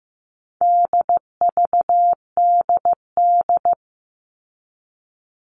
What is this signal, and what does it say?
Morse code, heard as a keyed tone, about 15 wpm, 700 Hz. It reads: DVDD
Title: All Cut Numbers - 4-Digit 15wpm